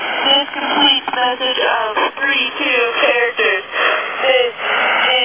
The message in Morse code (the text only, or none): none